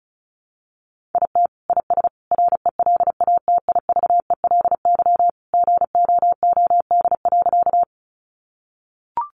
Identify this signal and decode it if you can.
Morse code, heard as a keyed tone, about 34 wpm, 700 Hz. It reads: IT IS RELATIVELY GOOD.
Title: Sentences from Top 1100 Words 35wpm